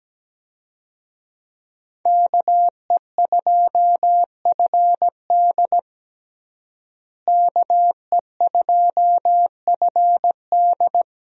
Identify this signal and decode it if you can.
Morse code, heard as a keyed tone, about 17 wpm, 700 Hz. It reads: KE2FD KE2FD
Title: USA Call Signs - 5 Characters - Repeated 2x - 17wpm